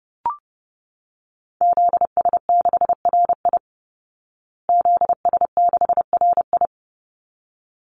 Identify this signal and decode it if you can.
Morse code, heard as a keyed tone, about 30 wpm, 700 Hz. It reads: ZS6RI ZS6RI
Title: Call Signs Repeated 3x - 30wpm